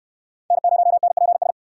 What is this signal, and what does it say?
Morse code, heard as a keyed tone, about 60 wpm, 700 Hz. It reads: N0NPS